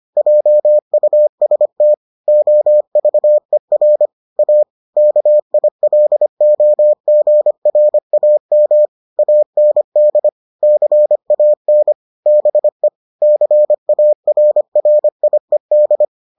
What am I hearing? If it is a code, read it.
Morse code, heard as a keyed tone, about 25 wpm, 600 Hz. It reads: JUST OVER A KILOGRAM AND CAN BE CARRIED